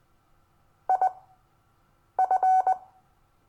Morse code, heard as a keyed tone, about 20 wpm, spaced as for 7 wpm, 750 Hz. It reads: IF
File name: cw / Lesson22 REVIEW